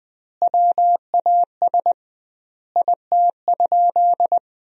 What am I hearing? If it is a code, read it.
Morse code, heard as a keyed tone, about 20 wpm, 700 Hz. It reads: WAS IT?
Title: 2x Word-Spacing - Sentences from Top 300 Words - 20wpm